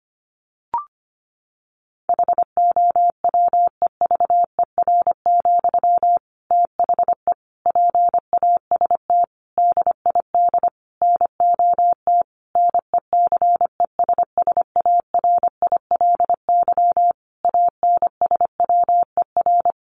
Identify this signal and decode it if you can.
Morse code, heard as a keyed tone, about 25 wpm, 700 Hz. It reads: HOWEVER, THE PAST DID NOT NECESSARILY ANSWER
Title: Sentences from Top 1400 Words 25wpm